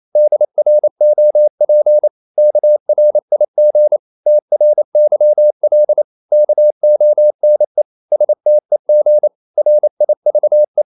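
Morse code, heard as a keyed tone, about 28 wpm, 600 Hz. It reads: DROP KRIG TRYL KONE STEG RIVE
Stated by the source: Danish short words sent in Morse at 28 wpm, from CW med Gnister.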